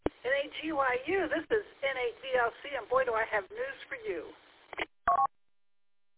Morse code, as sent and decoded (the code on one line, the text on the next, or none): none